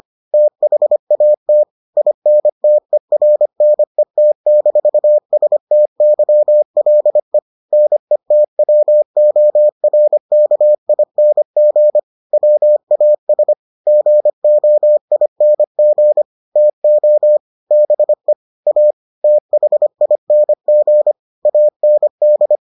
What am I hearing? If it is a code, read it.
Morse code, heard as a keyed tone, about 25 wpm, 600 Hz. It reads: THAT INTERNET-STYLE NETWORKING WAS GOING TO BE A THING AND